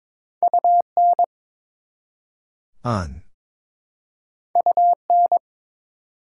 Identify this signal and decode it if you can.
Morse code, heard as a keyed tone, about 22 wpm, 700 Hz. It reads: UN UN
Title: Prefixes and Suffixes 22wpm